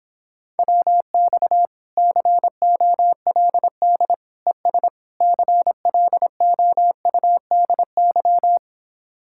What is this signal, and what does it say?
Morse code, heard as a keyed tone, about 26 wpm, 700 Hz. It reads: WX COLD ES CLOUDY